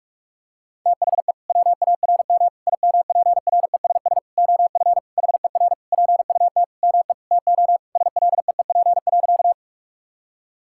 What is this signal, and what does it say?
Morse code, heard as a keyed tone, about 46 wpm, 700 Hz. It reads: THE WARM IMPRESS OF HER PUT ME TO SLEEP.
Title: Sentences from Top 1600 Words 45wpm